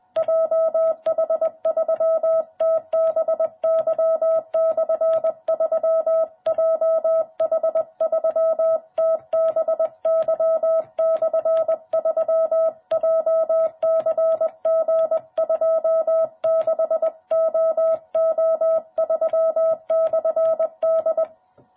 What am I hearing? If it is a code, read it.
Morse code, heard as a keyed tone, about 20 wpm, 660 Hz. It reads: JH3TBY/3 JH3TBY/3 JCG26OO3/D
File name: jh8cbh / 2021cw